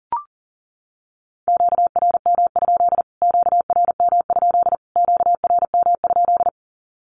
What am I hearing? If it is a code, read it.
Morse code, heard as a keyed tone, about 40 wpm, 700 Hz. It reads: QRM? QRM? QRM?
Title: Q-Codes Repeated 3x 40wpm